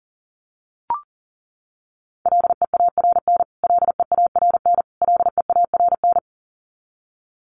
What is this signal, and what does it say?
Morse code, heard as a keyed tone, about 40 wpm, 700 Hz. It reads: LEARN LEARN LEARN
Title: Top 300 Words Repeated 40wpm